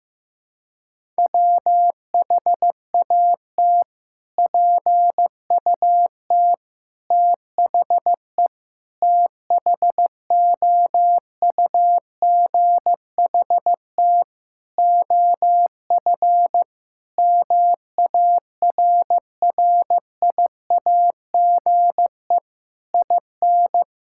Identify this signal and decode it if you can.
Morse code, heard as a keyed tone, about 15 wpm, 700 Hz. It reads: WHAT PUT THE THOUGHT OF MARRIAGE IN
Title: Sentences from Top 1000 Words - Review Part 5 - 15wpm